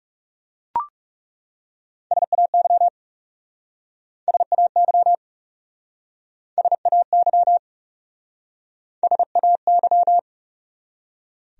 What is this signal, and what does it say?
Morse code, heard as a keyed tone, about 40 wpm, 700 Hz. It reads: SAY SAY SAY SAY